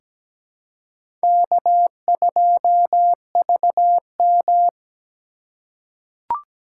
Morse code, heard as a keyed tone, about 17 wpm, 700 Hz. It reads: K2VM
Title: USA Call Signs - 3x Character Spacing - 17wpm